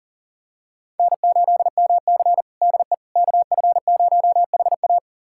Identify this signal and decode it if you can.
Morse code, heard as a keyed tone, about 40 wpm, 700 Hz. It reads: N8MC DE KF0HA
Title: USA Callsign DE Callsign 40wpm